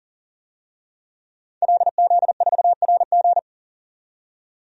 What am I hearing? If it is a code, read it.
Morse code, heard as a keyed tone, about 40 wpm, 700 Hz. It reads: LZ4RG
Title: DXCC-100 Call Signs - 5 Characters - 40wpm